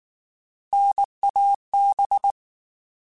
Morse code, heard as a keyed tone, about 19 wpm, 800 Hz. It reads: NAB
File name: Alphabetical / Alphabetical mp3s / Ns @18